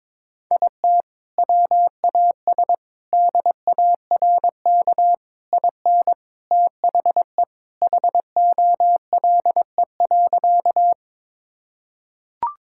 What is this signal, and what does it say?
Morse code, heard as a keyed tone, about 22 wpm, 700 Hz. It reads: IT WAS DARK IN THE HOLE.
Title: Sentences from Top 1600 Words 22wpm